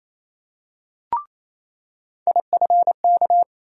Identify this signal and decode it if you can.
Morse code, heard as a keyed tone, about 28 wpm, 700 Hz. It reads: IFK
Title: USA Call Sign Postfixes - 3 Letters 28wpm